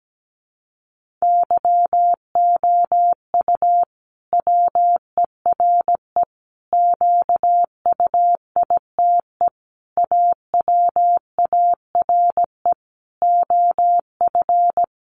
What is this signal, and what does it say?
Morse code, heard as a keyed tone, about 17 wpm, 700 Hz. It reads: YOU WERE QUITE AWARE OF
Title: Sentences from Top 1200 Words 17wpm